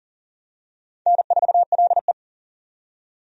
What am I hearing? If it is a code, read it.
Morse code, heard as a keyed tone, about 40 wpm, 700 Hz. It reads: N4LE
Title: USA Call Signs - 4 Characters - 40wpm